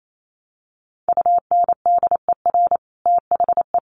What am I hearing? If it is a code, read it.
Morse code, heard as a keyed tone, about 28 wpm, 700 Hz. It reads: UNDER THE
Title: Sets of 5 Words Encore2 28wpm